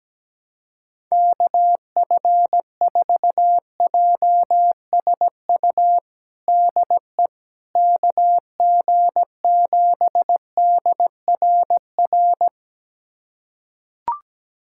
Morse code, heard as a keyed tone, about 17 wpm, 700 Hz. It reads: KF4JSU DE KG7DRR
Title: USA Callsign DE Callsign 17wpm